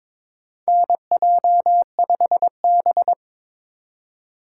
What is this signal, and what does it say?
Morse code, heard as a keyed tone, about 22 wpm, 700 Hz. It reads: NJ5B
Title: USA Call Signs - 4 Characters - 3x Character Spacing - 22wpm